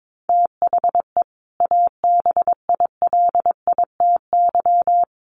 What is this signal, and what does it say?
Morse code, heard as a keyed tone, about 22 wpm, 700 Hz. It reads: THE ABILITY
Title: Sets of 3 Words 22wpm